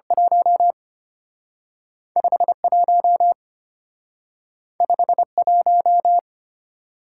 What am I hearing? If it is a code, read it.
Morse code, heard as a keyed tone, about 34 wpm, 700 Hz. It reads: 1 51 51